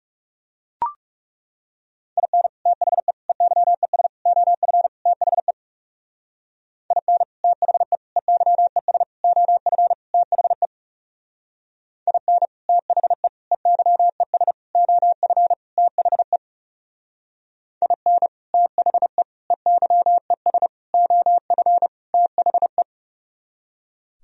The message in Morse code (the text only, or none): IN THE EYES OF THE IN THE EYES OF THE IN THE EYES OF THE IN THE EYES OF THE